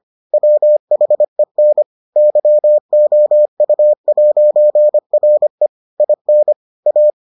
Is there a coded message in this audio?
Yes: Morse code from 0.3 to 7.1 s, heard as a keyed tone, about 25 wpm, 600 Hz. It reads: WHEN YOU'RE IN A